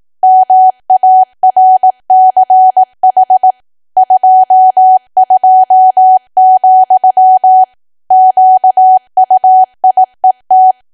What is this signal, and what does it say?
Morse code, heard as a keyed tone, about 18 wpm, 750 Hz. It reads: MARCH 22, QUIET
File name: cwa / W1AW-Level2 / 18WPM